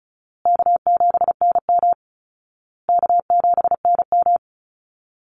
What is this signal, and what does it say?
Morse code, heard as a keyed tone, about 34 wpm, 700 Hz. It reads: K7NM K7NM
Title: Easy Call Signs Repeated 3x - 35wpm